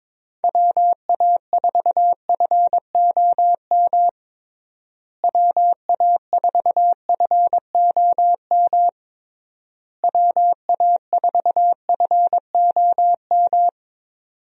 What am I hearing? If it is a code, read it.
Morse code, heard as a keyed tone, about 22 wpm, 700 Hz. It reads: WA4FOM WA4FOM WA4FOM
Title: USA Call Signs - 6 Characters - Repeated 3x - 22wpm